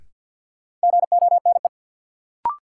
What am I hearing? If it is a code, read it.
Morse code, heard as a keyed tone, about 52 wpm, 700 Hz. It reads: GONE
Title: Sentences from Top 1200 Words - New Word Drill - 50wpm